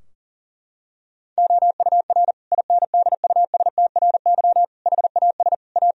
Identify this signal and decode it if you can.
Morse code, heard as a keyed tone, about 40 wpm, 700 Hz. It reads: OUR INDUSTRY HAS A
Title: Sentences from Top 1000 Words - Review Part 4 - 40wpm